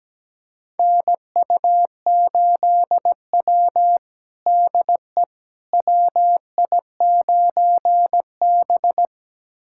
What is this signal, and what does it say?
Morse code, heard as a keyed tone, about 17 wpm, 700 Hz. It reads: NU8W DE WI9B
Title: USA Callsign DE Callsign - 4 Characters - 17wpm